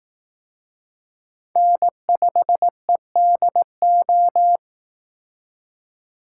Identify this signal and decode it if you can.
Morse code, heard as a keyed tone, about 18 wpm, 700 Hz. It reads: N5EDO
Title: Speed Racing - USA Call Signs 15wpm